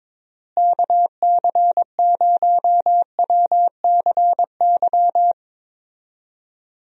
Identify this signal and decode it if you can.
Morse code, heard as a keyed tone, about 22 wpm, 700 Hz. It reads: KC0WCY
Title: USA Call Signs - 6 Characters - Repeated 3x - 22wpm